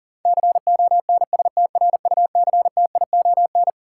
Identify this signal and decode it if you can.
Morse code, heard as a keyed tone, about 40 wpm, 700 Hz. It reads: CONSTRUCTION
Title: Sentences from Top 1600 Words - New Word Drill Alt - 40wpm